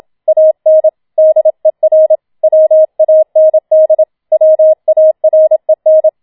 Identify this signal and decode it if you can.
Morse code, heard as a keyed tone, about 27 wpm, 630 Hz. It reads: AN DER WAND WAREN